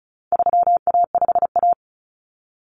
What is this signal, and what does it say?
Morse code, heard as a keyed tone, about 34 wpm, 700 Hz. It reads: 3A5A
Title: Common Cut Numbers - 4-Digit 35wpm